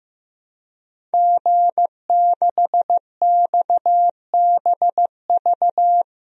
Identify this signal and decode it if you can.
Morse code, heard as a keyed tone, about 15 wpm, 700 Hz. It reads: G6XBV
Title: DXCC-100 Call Signs - 5 Characters - 15wpm